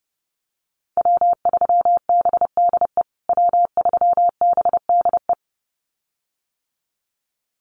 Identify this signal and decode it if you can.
Morse code, heard as a keyed tone, about 30 wpm, 700 Hz. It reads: W3BDE W3BDE
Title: USA Call Signs Repeated 2x 30wpm